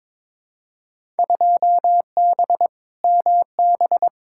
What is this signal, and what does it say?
Morse code, heard as a keyed tone, about 22 wpm, 700 Hz. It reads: 2B MB
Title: Field Day Exchanges Encore - 22wpm